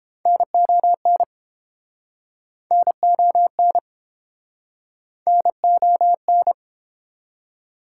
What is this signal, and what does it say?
Morse code, heard as a keyed tone, about 30 wpm, 700 Hz. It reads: NON NON NON